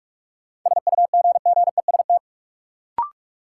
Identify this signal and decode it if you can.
Morse code, heard as a keyed tone, about 43 wpm, 700 Hz. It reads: SUGGEST
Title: Sentences from Top 1300 Words - New Word Drill Alt - 45wpm